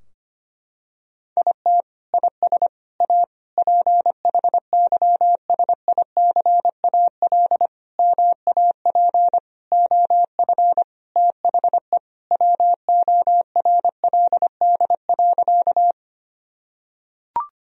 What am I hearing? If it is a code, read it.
Morse code, heard as a keyed tone, about 25 wpm, 700 Hz. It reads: IT IS A PHYSICAL MAP OF THE WORLD.
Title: Sentences from Top 1000 Words - Review Part 4 - 25wpm